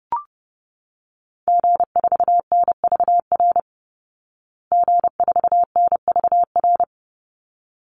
Morse code, heard as a keyed tone, about 30 wpm, 700 Hz. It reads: G4NVR G4NVR
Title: Call Signs Repeated 3x - 30wpm